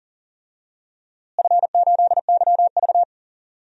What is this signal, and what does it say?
Morse code, heard as a keyed tone, about 40 wpm, 700 Hz. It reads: F8YV
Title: DXCC-100 Call Signs - 4 Characters - Repeated 3x - 40wpm